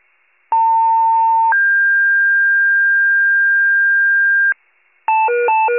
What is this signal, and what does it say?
Two-tone page, heard as a one-second tone, then a three-second tone, about 900 then 1600 Hz, with noise. It is heard from 0.5 to 4.5 s.